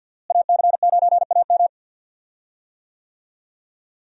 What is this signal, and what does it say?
Morse code, heard as a keyed tone, about 52 wpm, 700 Hz. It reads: AC9AM
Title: USA Call Signs - 5 Characters - Repeated 2x - 50wpm